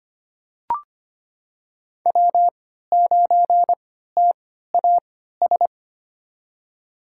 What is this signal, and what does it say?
Morse code, heard as a keyed tone, about 25 wpm, spaced as for 14 wpm, 700 Hz. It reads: W9TAS
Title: USA Call Signs - 5 Characters - 3x Character Spacing - 25wpm